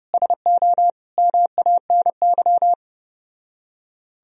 Morse code, heard as a keyed tone, about 32 wpm, 700 Hz. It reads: SO MANY